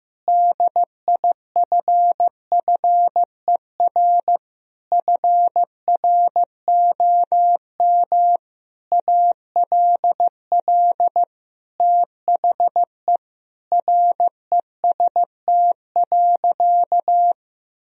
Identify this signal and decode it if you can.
Morse code, heard as a keyed tone, about 15 wpm, 700 Hz. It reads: DIFFER FROM ALL THE REST.
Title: Sentences from Top 1000 Words - Review Part 1 - 15wpm